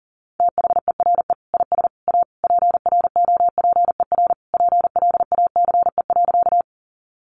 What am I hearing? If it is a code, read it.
Morse code, heard as a keyed tone, about 40 wpm, 690 Hz. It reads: THERE IS A PROPER PLACE.
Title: Sentences from Top 800 Words 40wpm